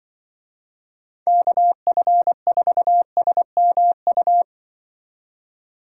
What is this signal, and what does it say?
Morse code, heard as a keyed tone, about 24 wpm, 700 Hz. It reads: KF4SMU
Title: Speed Racing - USA 2x3 Call Signs 20wpm